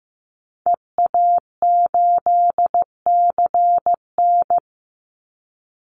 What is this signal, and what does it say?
Morse code, heard as a keyed tone, about 15 wpm, 700 Hz. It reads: EA8CN
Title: Call Signs Repeated 2x - 15wpm